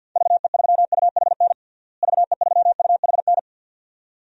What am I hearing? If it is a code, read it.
Morse code, heard as a keyed tone, about 49 wpm, 700 Hz. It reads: VE3UHN VE3UHN
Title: DXCC-100 Call Signs - 6 Characters - Repeated 2x - 50wpm